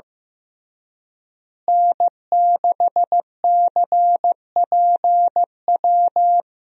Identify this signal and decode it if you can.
Morse code, heard as a keyed tone, about 16 wpm, 700 Hz. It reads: N6CPW
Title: USA Call Signs - 5 Characters - Repeated 2x - 15wpm